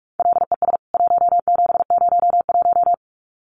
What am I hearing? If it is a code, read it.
Morse code, heard as a keyed tone, about 44 wpm, 700 Hz. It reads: LES 1701
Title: CWT Exchange 45wpm